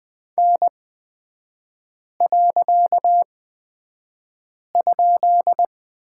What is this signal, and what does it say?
Morse code, heard as a keyed tone, about 20 wpm, spaced as for 5 wpm, 700 Hz. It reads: N.?
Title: 255 Lesson 41e - Period - Three Characters 20wpm